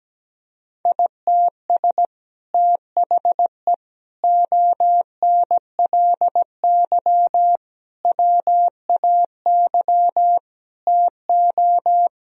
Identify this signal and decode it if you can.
Morse code, heard as a keyed tone, about 17 wpm, 700 Hz. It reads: ITS THE ONLY WAY TO
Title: Sentences from Top 1000 Words - Review Part 2 - 17wpm